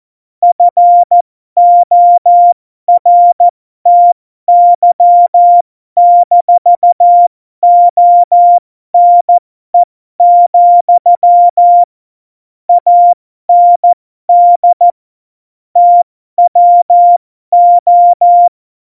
Morse code, heard as a keyed tone, about 14 wpm, spaced as for 12 wpm, 700 Hz. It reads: FORTY-ONE, AND TWO
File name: Morsekurs / Witze